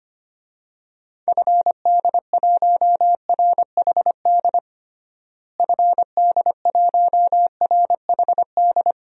Morse code, heard as a keyed tone, about 25 wpm, 700 Hz. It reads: FD1RHD FD1RHD
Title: DXCC-100 Call Signs - 6 Characters - Repeated 2x - 25wpm